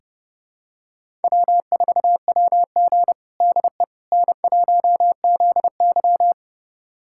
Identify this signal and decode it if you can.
Morse code, heard as a keyed tone, about 30 wpm, 700 Hz. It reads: W4WG DE N1ZY
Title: USA Callsign DE Callsign - 4 Characters - 30wpm